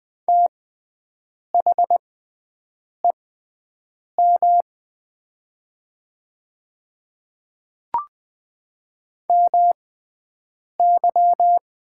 Morse code, heard as a keyed tone, about 20 wpm, spaced as for 7 wpm, 700 Hz. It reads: THEM MY
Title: 6x Character Spacing - Top 100 Words - 20wpm